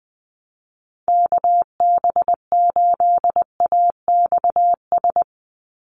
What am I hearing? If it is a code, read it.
Morse code, heard as a keyed tone, about 20 wpm, 700 Hz. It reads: KB8AXS
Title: USA Call Signs 20wpm